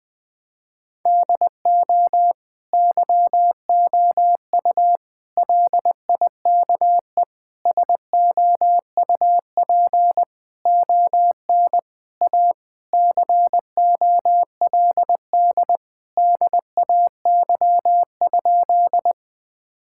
Morse code, heard as a keyed tone, about 20 wpm, 700 Hz. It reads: DO YOU LIKE SOUP ON A COLD DAY?
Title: Sentences from Top 1000 Words - Review Part 6 - 20wpm